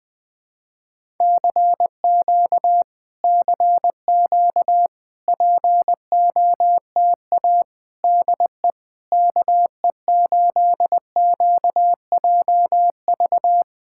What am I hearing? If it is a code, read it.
Morse code, heard as a keyed tone, about 20 wpm, 700 Hz. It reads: CQ CQ POTA DE KE8QJV
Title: POTA QSO Practice - 20wpm